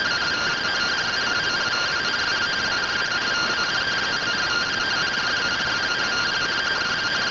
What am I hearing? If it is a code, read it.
Two-tone teleprinter signal, about 170 Hz shift.